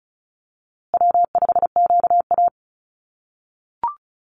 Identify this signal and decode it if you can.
Morse code, heard as a keyed tone, about 34 wpm, 700 Hz. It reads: W5QA
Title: USA Call Signs Repeated 2x 35wpm